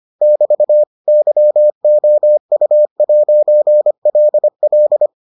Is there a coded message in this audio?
Yes: Morse code from 0.2 to 5.1 s, heard as a keyed tone, about 25 wpm, 600 Hz. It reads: <BT> YOU'LL